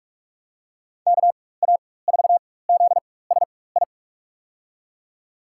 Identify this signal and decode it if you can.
Morse code, heard as a keyed tone, about 41 wpm, spaced as for 20 wpm, 700 Hz. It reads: KA4ZSI
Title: USA Call Signs - 6 Characters - 4x Character Spacing - 45wpm